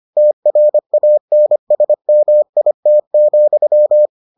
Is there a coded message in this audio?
Yes: Morse code from 0.2 to 4.1 s, heard as a keyed tone, about 25 wpm, 600 Hz. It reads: TRANSMIT,